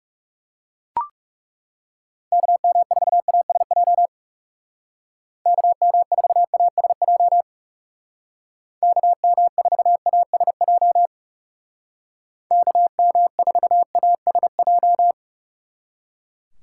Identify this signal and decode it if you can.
Morse code, heard as a keyed tone, about 37 wpm, 700 Hz. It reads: KM4ASJ KM4ASJ KM4ASJ KM4ASJ
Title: Speed Racing - USA 2x3 Call Signs 30wpm